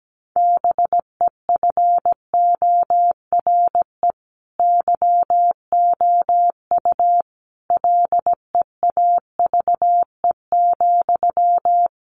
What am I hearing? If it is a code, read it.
Morse code, heard as a keyed tone, about 17 wpm, 700 Hz. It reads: BEFORE YOU LEAVE,
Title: Sentences from Top 400 Words 17wpm